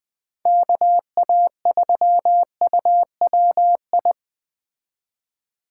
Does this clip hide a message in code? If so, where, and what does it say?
Yes: Morse code from 0.5 to 4.1 s, heard as a keyed tone, about 20 wpm, 700 Hz. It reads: KA3UWI